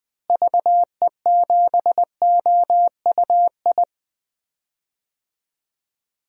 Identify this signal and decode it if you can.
Morse code, heard as a keyed tone, about 20 wpm, 700 Hz. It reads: VE7OUI
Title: DXCC-100 Call Signs - 6 Characters - Repeated 3x - 20wpm